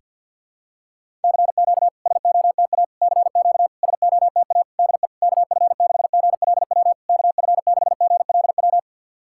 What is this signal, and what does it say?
Morse code, heard as a keyed tone, about 50 wpm, 700 Hz. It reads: CQ SOTA CQ SOTA DE KF6GLW KF6GLW
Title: SOTA QSO Practice 50wpm